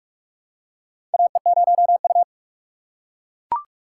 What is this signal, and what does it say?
Morse code, heard as a keyed tone, about 45 wpm, 700 Hz. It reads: AE0U
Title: USA Call Signs - 4 Characters - 6x Character Spacing - 45wpm